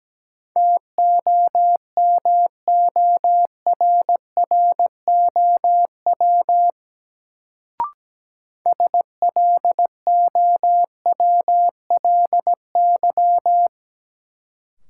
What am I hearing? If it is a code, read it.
Morse code, heard as a keyed tone, about 17 wpm, 700 Hz. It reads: TOMORROW SLOWLY
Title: Sentences from Top 1100 Words - New Word Drill - 17wpm